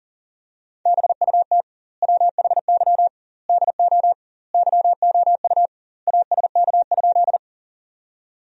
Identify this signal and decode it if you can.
Morse code, heard as a keyed tone, about 37 wpm, 700 Hz. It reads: BUT WHY DO YOU ASK?
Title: 2x Word-Spacing - Sentences from Top 200 Words - 40wpm